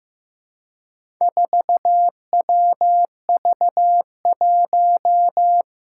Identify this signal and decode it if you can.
Morse code, heard as a keyed tone, about 15 wpm, 700 Hz. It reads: 4WV1
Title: Four Challenging Letter-Number Combinations 15wpm